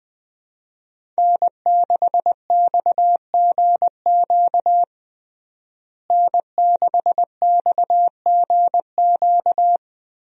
Morse code, heard as a keyed tone, about 20 wpm, 700 Hz. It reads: N6XGQ N6XGQ
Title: USA Call Signs - 5 Characters - Repeated 3x - 20wpm